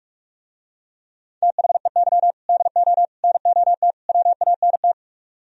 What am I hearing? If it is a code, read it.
Morse code, heard as a keyed tone, about 49 wpm, 690 Hz. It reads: THEY DO NOT WANT